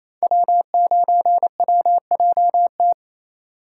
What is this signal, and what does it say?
Morse code, heard as a keyed tone, about 28 wpm, 700 Hz. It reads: W9WJT
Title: Code Groups - 5 Letter:Numbers - 28wpm